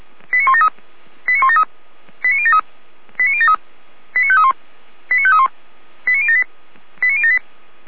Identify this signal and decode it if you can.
Five-tone selective call (ZVEI), heard as quick runs of five tones, tones of about 70 ms. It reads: 89173 89173 89073 89073 89531 89531 89078 89078